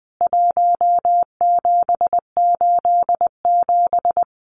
Morse code, heard as a keyed tone, about 20 wpm, 700 Hz. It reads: 1787
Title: CWT Exchange 20wpm